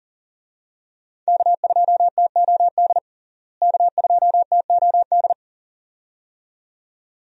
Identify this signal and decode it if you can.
Morse code, heard as a keyed tone, about 40 wpm, 700 Hz. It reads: K2TOD K2TOD
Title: USA Call Signs - 5 Characters - Repeated 2x - 40wpm